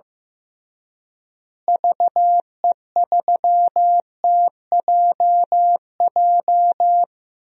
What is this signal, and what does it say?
Morse code, heard as a keyed tone, about 15 wpm, 700 Hz. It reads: VE3TJJ
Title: DXCC-100 Call Signs - 6 Characters - Repeated 3x - 15wpm